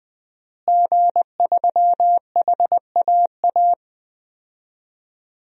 Morse code, heard as a keyed tone, about 20 wpm, 700 Hz. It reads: G3HAA
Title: DXCC-100 Call Signs - 5 Characters - Repeated 2x - 20wpm